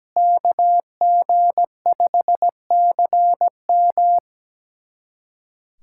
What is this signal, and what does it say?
Morse code, heard as a keyed tone, about 17 wpm, 700 Hz. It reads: KG5CM
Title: USA Callsign DE Callsign - 5 Characters - 17wpm